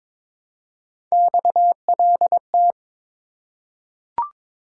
Morse code, heard as a keyed tone, about 22 wpm, 700 Hz. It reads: XLT
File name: practice / 22wpm / Three Letter-Number Combinations - 22wpm